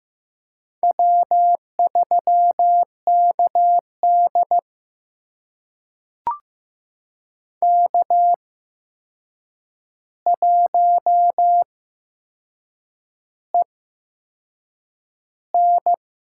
Morse code, heard as a keyed tone, about 15 wpm, spaced as for 6 wpm, 700 Hz. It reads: W3KD K1EN
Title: USA Call Signs - 4 Characters - 8x Character Spacing - 15wpm